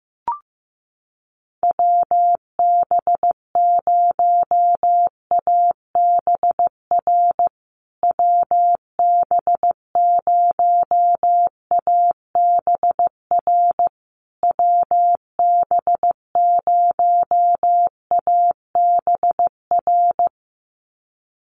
Morse code, heard as a keyed tone, about 15 wpm, 700 Hz. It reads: WB0ABR WB0ABR WB0ABR
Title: USA Call Signs Repeated 3x 15wpm